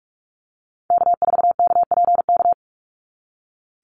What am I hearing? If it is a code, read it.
Morse code, heard as a keyed tone, about 44 wpm, 700 Hz. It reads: K4KPK
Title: NA SOTA Activator Callsigns 45wpm